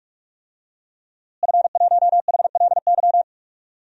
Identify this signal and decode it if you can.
Morse code, heard as a keyed tone, about 46 wpm, 700 Hz. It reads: F1HLY
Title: DXCC-100 Call Signs - 5 Characters - Repeated 3x - 45wpm